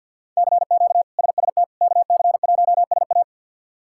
Morse code, heard as a keyed tone, about 53 wpm, 700 Hz. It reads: CQ SST KC1SA
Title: K1USN SST QSO Practice - 50wpm